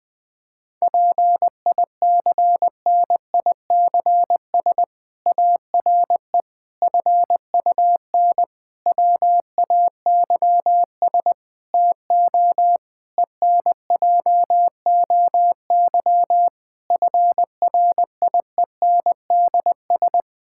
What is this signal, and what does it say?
Morse code, heard as a keyed tone, about 21 wpm, 700 Hz. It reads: PICNICS ARE FUN WAYS TO ENJOY FRIENDS
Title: Difficult Sentences with Hints - 20wpm